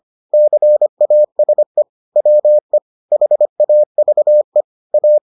Morse code, heard as a keyed tone, about 25 wpm, 600 Hz. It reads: CASE WE HAVE A